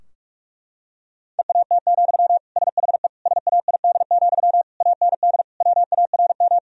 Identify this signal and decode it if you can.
Morse code, heard as a keyed tone, about 41 wpm, 700 Hz. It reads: EAT, SHE SAID, AND WARM
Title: Sentences from Top 1000 Words - Review Part 2 - 45wpm